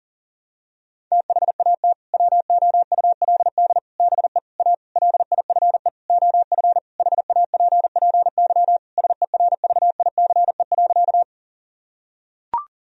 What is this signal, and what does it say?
Morse code, heard as a keyed tone, about 40 wpm, 700 Hz. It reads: THAT WOULD BE A LIFE OF HAPPY SERVICE.
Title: Sentences from Top 1000 Words - Review Part 4 - 40wpm